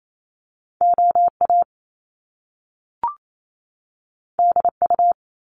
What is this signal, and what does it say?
Morse code, heard as a keyed tone, about 28 wpm, 700 Hz. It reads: OA DU
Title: Top Two Letter Combinations in Words 28wpm